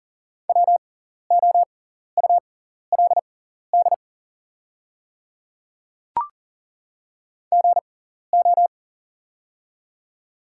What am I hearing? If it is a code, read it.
Morse code, heard as a keyed tone, about 39 wpm, spaced as for 14 wpm, 700 Hz. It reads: WOULD GO
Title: 6x Character Spacing - Top 100 Words - 40wpm